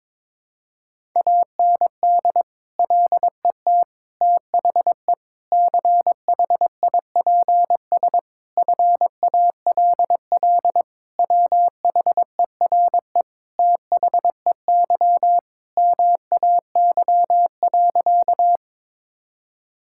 Morse code, heard as a keyed tone, about 22 wpm, 700 Hz. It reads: AND LET THE CHIPS FALL WHERE THEY MAY.
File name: practice / 22wpm / Sentences with 8-Word Phrases - 22wpm